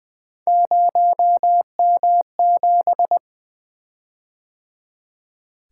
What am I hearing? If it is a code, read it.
Morse code, heard as a keyed tone, about 20 wpm, 700 Hz. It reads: 0M7
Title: Three Letter-Number Combinations - 20wpm